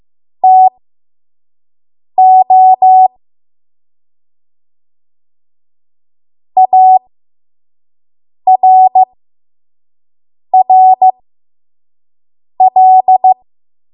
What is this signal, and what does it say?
Morse code, heard as a keyed tone, about 15 wpm, spaced as for 5 wpm, 750 Hz. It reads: TO ARRL
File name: Morse / 05WPM